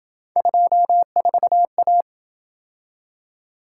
Morse code, heard as a keyed tone, about 26 wpm, 700 Hz. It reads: 24A